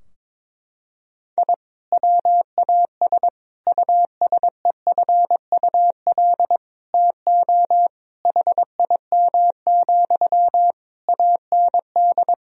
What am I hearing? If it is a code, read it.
Morse code, heard as a keyed tone, about 22 wpm, 700 Hz. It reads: I WAS USEFUL TO HIM, AND
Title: Sentences from Top 1000 Words - Review Part 4 - 22wpm